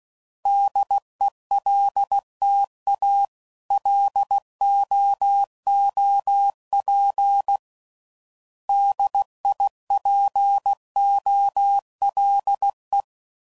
Morse code, heard as a keyed tone, about 16 wpm, 800 Hz. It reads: DELTALOOP DIPOLE
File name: CW Code Groups @15WPM / fromSM's / Antennas